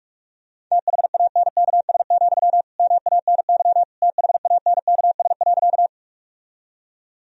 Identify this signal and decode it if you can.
Morse code, heard as a keyed tone, about 46 wpm, 700 Hz. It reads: THANKS, MANY THANKS.
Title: Sentences from Top 1000 Words - Review Part 3 - 45wpm